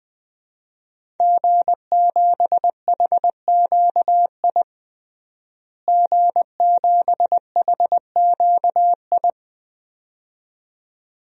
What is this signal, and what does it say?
Morse code, heard as a keyed tone, about 21 wpm, 700 Hz. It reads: G7HQI G7HQI
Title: DXCC-100 Call Signs - 5 Characters - Repeated 2x - 20wpm